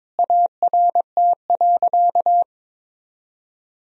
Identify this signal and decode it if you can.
Morse code, heard as a keyed tone, about 22 wpm, 700 Hz. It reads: ART.
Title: Sentences from Top 1000 Words - Review Part 3 - 22wpm